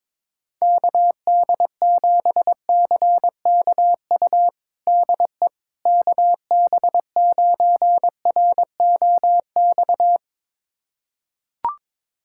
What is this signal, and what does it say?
Morse code, heard as a keyed tone, about 21 wpm, 700 Hz. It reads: KD7CKU DE KB9ROX
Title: USA Callsign DE Callsign 22wpm